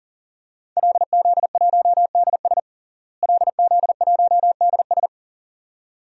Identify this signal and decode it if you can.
Morse code, heard as a keyed tone, about 37 wpm, 700 Hz. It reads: LZ1DS LZ1DS
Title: DXCC-100 Call Signs - 5 Characters - Repeated 2x - 40wpm